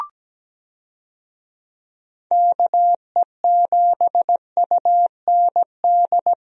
Morse code, heard as a keyed tone, about 17 wpm, 700 Hz. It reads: KE7UND
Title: USA Call Signs - 6 Characters - Repeated 3x - 17wpm